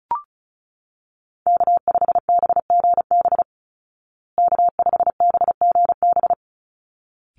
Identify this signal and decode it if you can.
Morse code, heard as a keyed tone, about 34 wpm, 700 Hz. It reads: K5BGB K5BGB
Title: Call Signs Repeated 3x - 35wpm